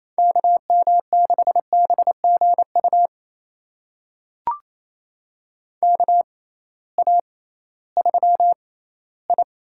Morse code, heard as a keyed tone, about 29 wpm, spaced as for 17 wpm, 700 Hz. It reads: KM6BGU KA3I
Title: USA Call Signs - 6x Character Spacing - 28wpm